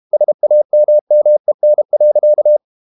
Morse code, heard as a keyed tone, about 32 wpm, 600 Hz. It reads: SAMMEN.
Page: Ferie 32 wpm